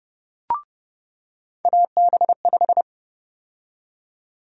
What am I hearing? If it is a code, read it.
Morse code, heard as a keyed tone, about 31 wpm, 700 Hz. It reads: AB5
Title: USA Call Sign Prefixes - 30wpm